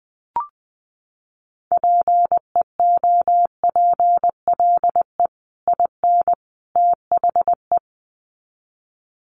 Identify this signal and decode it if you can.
Morse code, heard as a keyed tone, about 19 wpm, 700 Hz. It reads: PEOPLE IN THE
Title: Sets of 3 Words 20wpm